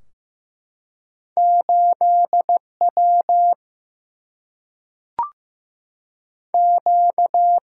Morse code, heard as a keyed tone, about 16 wpm, 700 Hz. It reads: 8 W Q
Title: Single-Letter Single-Number 15wpm